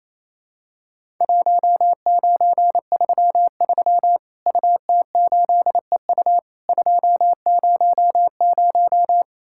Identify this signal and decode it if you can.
Morse code, heard as a keyed tone, about 28 wpm, 700 Hz. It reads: 1933 UT8EU 200
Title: Worked All Europe DX Contest - QTC Exchange - 28wpm